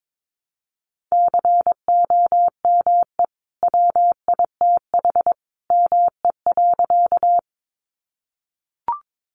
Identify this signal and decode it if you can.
Morse code, heard as a keyed tone, about 22 wpm, 700 Hz. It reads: COME WITH ME.
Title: Sentences from Top 200 Words 22wpm